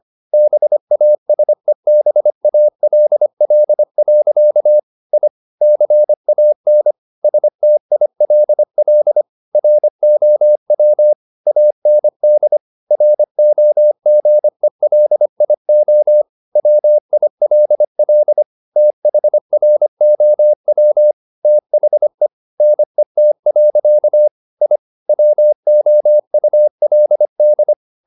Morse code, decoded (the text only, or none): BASEBALL. I CAN STILL ROW AND ROGELIO WILL THROW THE NET. I WOULD